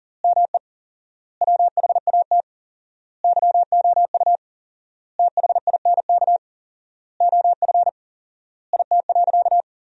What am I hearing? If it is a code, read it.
Morse code, heard as a keyed tone, about 40 wpm, 700 Hz. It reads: ME WHAT YOU THINK OF IT.
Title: 4x Word-Spacing - Sentences from Top 200 Words - 40wpm